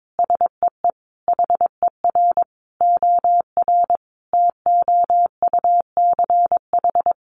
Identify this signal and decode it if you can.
Morse code, heard as a keyed tone, about 22 wpm, 700 Hz. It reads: SEE HER OR TOUCH
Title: Sentences from Top 1200 Words 22wpm